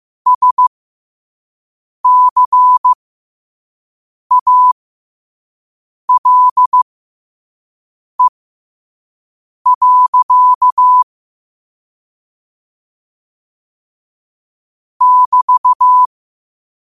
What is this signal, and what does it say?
Morse code, heard as a keyed tone, about 15 wpm, spaced as for 5 wpm, 1000 Hz. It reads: SCALE. <BT>
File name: qotdfiles / 5wpm